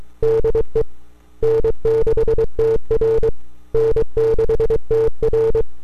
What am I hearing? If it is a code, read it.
Morse code, heard as a keyed tone, about 23 wpm, 450 Hz. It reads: DE N6TR N6TR
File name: n6tr / 12Oct2009-160